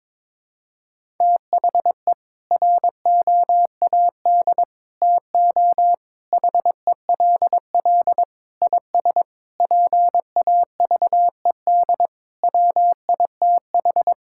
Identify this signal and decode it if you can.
Morse code, heard as a keyed tone, about 22 wpm, 700 Hz. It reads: THE ROAD TO HELL IS PAVED WITH
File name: practice / 22wpm / Sets of 8 Words - 22wpm